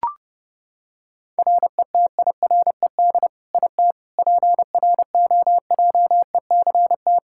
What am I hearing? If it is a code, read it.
Morse code, heard as a keyed tone, about 30 wpm, 700 Hz. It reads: RETIRED IT PROJECT
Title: Practice QSO - Exchange 3 - More Info - 30wpm